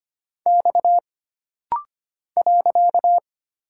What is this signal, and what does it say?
Morse code, heard as a keyed tone, about 25 wpm, 700 Hz. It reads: X .
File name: Single Character (Letter:Number:Punctuation:Prosign) - Rapid-Fire - 25wpm